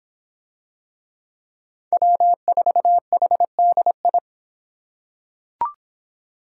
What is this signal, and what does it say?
Morse code, heard as a keyed tone, about 26 wpm, 700 Hz. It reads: W4HDI